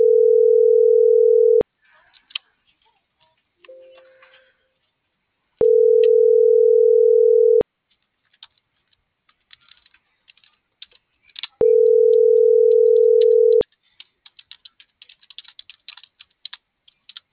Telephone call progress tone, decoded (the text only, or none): ringback tone